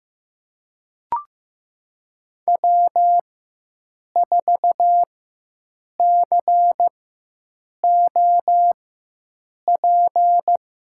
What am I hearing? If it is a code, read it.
Morse code, heard as a keyed tone, about 15 wpm, spaced as for 7 wpm, 700 Hz. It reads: W4COP
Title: USA Call Signs - 5 Characters - 4x Character Spacing - 15wpm